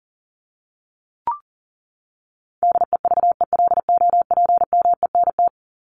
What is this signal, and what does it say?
Morse code, heard as a keyed tone, about 40 wpm, 700 Hz. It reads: DEVELOPMENT
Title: Top 501-750 Words 40wpm